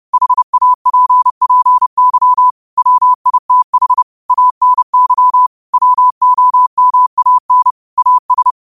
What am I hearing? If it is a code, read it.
Morse code, heard as a keyed tone, about 30 wpm, 1000 Hz. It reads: HAPPY WITH ANY WOMAN AS